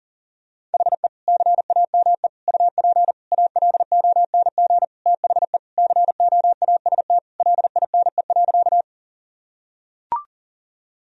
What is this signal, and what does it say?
Morse code, heard as a keyed tone, about 40 wpm, 700 Hz. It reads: HE CAME UP ALONG THE COAST LINE.
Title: Sentences from Top 1000 Words - Review Part 4 - 40wpm